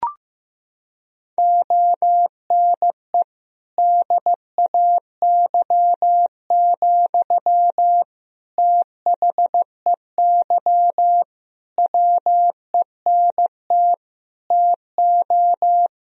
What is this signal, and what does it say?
Morse code, heard as a keyed tone, about 16 wpm, 700 Hz. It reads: ONE DAY, THEY WENT TO
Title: Sentences from Top 1000 Words - Review Part 2 - 15wpm